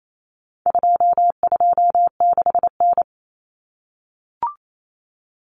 Morse code, heard as a keyed tone, about 28 wpm, 700 Hz. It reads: 226N
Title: Common Cut Numbers - 4-Digit 28wpm